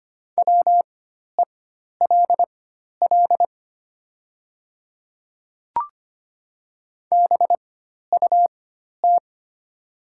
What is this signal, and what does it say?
Morse code, heard as a keyed tone, about 25 wpm, spaced as for 12 wpm, 700 Hz. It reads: WELL BUT
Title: 4x Character Spacing - Top 200 Words - 25wpm